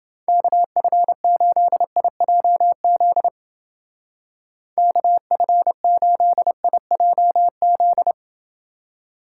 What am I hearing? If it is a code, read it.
Morse code, heard as a keyed tone, about 29 wpm, 700 Hz. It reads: KF8IJZ KF8IJZ